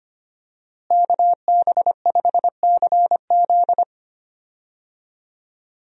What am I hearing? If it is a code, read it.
Morse code, heard as a keyed tone, about 25 wpm, 700 Hz. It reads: KB5CZ
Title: USA Call Signs - 5 Characters - Repeated 2x - 25wpm